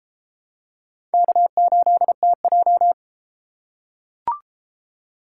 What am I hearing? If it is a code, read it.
Morse code, heard as a keyed tone, about 32 wpm, 700 Hz. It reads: K8TJ